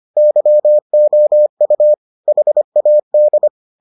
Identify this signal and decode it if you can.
Morse code, heard as a keyed tone, about 25 wpm, 600 Hz. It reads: YOU HAD